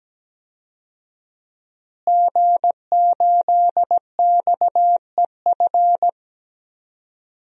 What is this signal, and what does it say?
Morse code, heard as a keyed tone, about 17 wpm, 700 Hz. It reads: G8XEF
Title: DXCC-100 Call Signs - 5 Characters - Repeated 2x - 17wpm